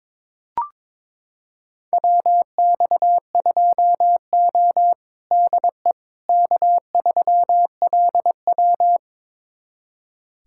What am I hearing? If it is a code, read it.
Morse code, heard as a keyed tone, about 21 wpm, 700 Hz. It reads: WX2O DE K3LW
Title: USA Callsign DE Callsign - 4 Characters - 22wpm